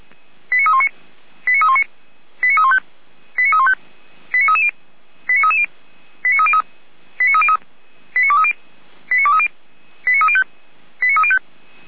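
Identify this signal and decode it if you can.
Five-tone selective call (ZVEI), heard as quick runs of five tones, tones of about 70 ms. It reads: 89319 89319 89315 89315 89330 89330 89303 89303 89230 89230 89305 89305
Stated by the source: ZVEI five-tone call 89125 received on an SDR.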